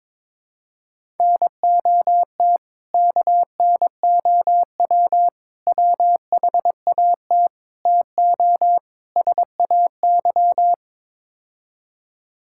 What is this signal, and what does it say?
Morse code, heard as a keyed tone, about 22 wpm, 700 Hz. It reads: NOT KNOW WHAT TO SAY
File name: practice / 22wpm / Build A Phrase - 5-Word Phrases - 22wpm